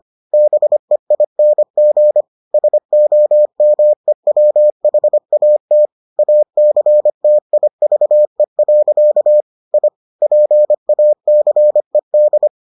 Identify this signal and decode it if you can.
Morse code, heard as a keyed tone, about 25 wpm, 600 Hz. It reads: BEING SOMEWHAT ACTIVE. I PACED